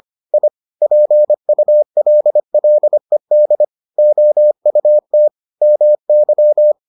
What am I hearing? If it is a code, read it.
Morse code, heard as a keyed tone, about 25 wpm, 600 Hz. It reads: I PULLED OUT MY